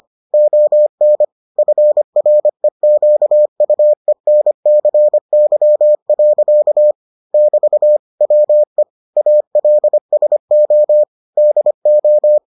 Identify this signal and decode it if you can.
Morse code, heard as a keyed tone, about 25 wpm, 600 Hz. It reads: ON FREQUENCY. <BT> WE ALSO DO